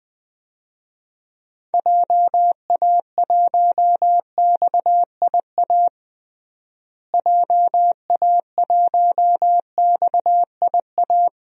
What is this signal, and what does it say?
Morse code, heard as a keyed tone, about 21 wpm, 700 Hz. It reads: JA1XIA JA1XIA
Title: DXCC-100 Call Signs - 6 Characters - Repeated 2x - 20wpm